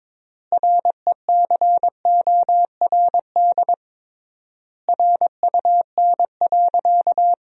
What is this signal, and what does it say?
Morse code, heard as a keyed tone, about 22 wpm, 700 Hz. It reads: RECORD RUN.
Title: 3x Word-Spacing - Sentences from Top 500 Words - 22wpm